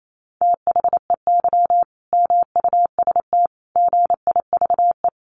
Morse code, heard as a keyed tone, about 28 wpm, 700 Hz. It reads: THEY MUST GIVE
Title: Sentences from Top 1100 Words 28wpm